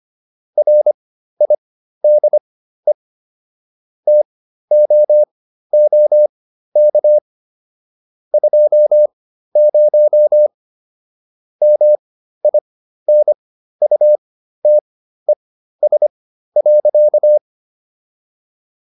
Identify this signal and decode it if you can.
Morse code, heard as a keyed tone, about 25 wpm, spaced as for 13 wpm, 600 Hz. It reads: RIDE TOOK 20 MINUTES.